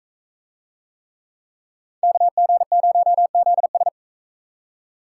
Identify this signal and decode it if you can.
Morse code, heard as a keyed tone, about 41 wpm, 700 Hz. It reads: KG0ZS